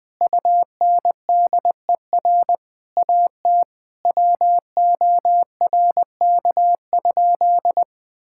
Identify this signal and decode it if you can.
Morse code, heard as a keyed tone, about 20 wpm, 700 Hz. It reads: UNDER AT WORK?
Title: Sentences from Top 1000 Words - Review Part 6 - 20wpm